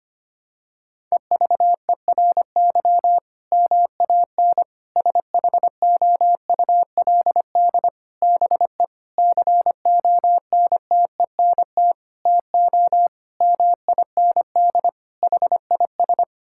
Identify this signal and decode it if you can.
Morse code, heard as a keyed tone, about 25 wpm, 700 Hz. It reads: EVERY MAN SHOULD BE CONTENT TO MIND HIS
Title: Sentences from Top 1000 Words - Review Part 6 - 25wpm